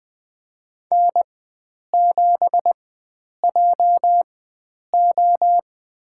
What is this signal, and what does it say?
Morse code, heard as a keyed tone, about 20 wpm, spaced as for 9 wpm, 700 Hz. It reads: N7JO
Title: USA Call Signs - 4 Characters - 4x Character Spacing - 20wpm